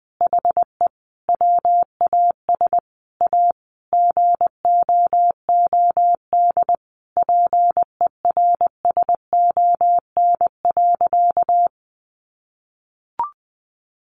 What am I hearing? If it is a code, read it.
Morse code, heard as a keyed tone, about 20 wpm, 700 Hz. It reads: HE WAS A GOOD PERSON.
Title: Sentences from Top 1300 Words 20wpm